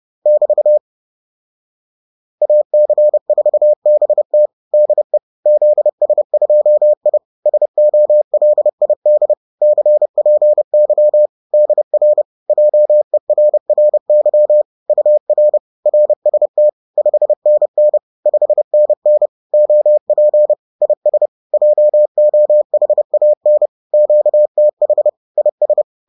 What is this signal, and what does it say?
Morse code, heard as a keyed tone, about 30 wpm, 600 Hz. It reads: <BT> AC4BT DE ZS2I SOLID CPY DR JERRY UR RST 5NN 5NN OP IS JOHAN QTH IS